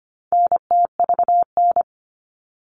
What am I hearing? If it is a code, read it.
Morse code, heard as a keyed tone, about 25 wpm, 700 Hz. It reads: NTVN
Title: All Cut Numbers - 4-Digit 25wpm